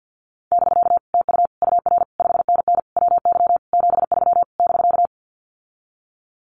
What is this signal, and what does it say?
Morse code, heard as a keyed tone, about 50 wpm, 700 Hz. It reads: <BK> TU UR 5NN WY 73 <BK>
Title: Common QSO Phrases Encore 50wpm